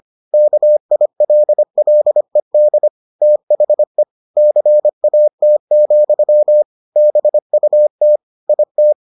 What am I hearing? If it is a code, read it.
Morse code, heard as a keyed tone, about 25 wpm, 600 Hz. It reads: KILLED THE CAT, BUT IT